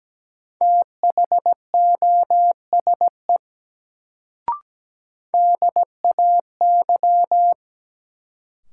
Morse code, heard as a keyed tone, about 17 wpm, 700 Hz. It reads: THOSE DAY
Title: Sets of 2 Words - New Word Drill - 17wpm